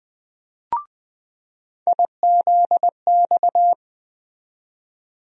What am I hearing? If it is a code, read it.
Morse code, heard as a keyed tone, about 20 wpm, 700 Hz. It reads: IZX
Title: USA Call Sign Postfixes - 3 Letters 20wpm